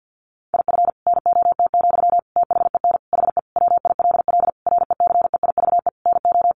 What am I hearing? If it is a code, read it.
Morse code, heard as a keyed tone, about 50 wpm, 700 Hz. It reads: IF NOT, THEN HE WILL RECEIVE NO
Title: Sentences from Top 1500 Words 50wpm